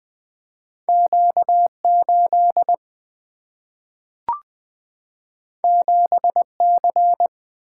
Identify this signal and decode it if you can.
Morse code, heard as a keyed tone, about 20 wpm, 700 Hz. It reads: Q8 7C
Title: Two Letter-Number Combinations - 20wpm